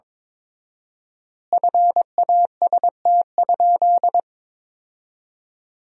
Morse code, heard as a keyed tone, about 22 wpm, 700 Hz. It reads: FAST?
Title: 4x Word-Spacing - Sentences from Top 500 Words - 22wpm